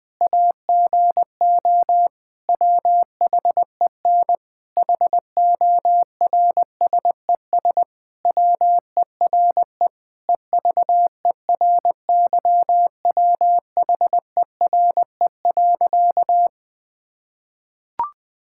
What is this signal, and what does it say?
Morse code, heard as a keyed tone, about 20 wpm, 700 Hz. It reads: AGO WHEN HORSES WERE EVERYWHERE.
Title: Sentences with 8-Word Phrases - 20wpm